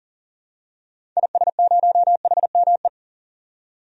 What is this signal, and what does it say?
Morse code, heard as a keyed tone, about 40 wpm, 700 Hz. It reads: IS0HME
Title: DXCC-100 Call Signs - 6 Characters - 40wpm